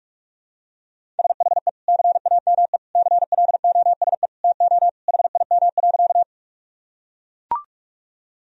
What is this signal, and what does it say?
Morse code, heard as a keyed tone, about 41 wpm, 700 Hz. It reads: SHE CAME CLOSE TO HIM.
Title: Sentences from Top 1000 Words - Review Part 1 - 45wpm